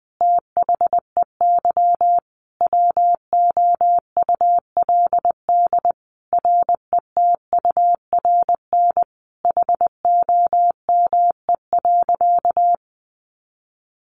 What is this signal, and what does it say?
Morse code, heard as a keyed tone, about 19 wpm, 700 Hz. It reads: THEY WOULD RETURN HOME.
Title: Sentences from Top 1400 Words 20wpm